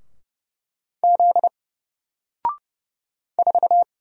Morse code, heard as a keyed tone, about 30 wpm, 700 Hz. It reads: Z 4
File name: Single-Letter-Number-30wpm